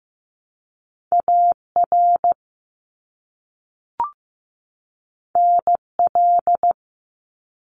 Morse code, heard as a keyed tone, about 15 wpm, 700 Hz. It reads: AR NL
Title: ARRL Sections 15wpm